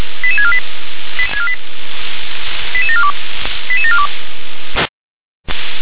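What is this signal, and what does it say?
Five-tone selective call (ZVEI), heard as quick runs of five tones, tones of about 70 ms. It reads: 88648 88742 88742
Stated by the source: ZVEI five-tone call 88747 received on an SDR.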